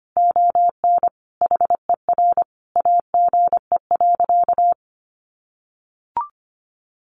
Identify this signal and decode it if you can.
Morse code, heard as a keyed tone, about 25 wpm, 700 Hz. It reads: ON HER AGE.
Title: Sentences from Top 1300 Words 25wpm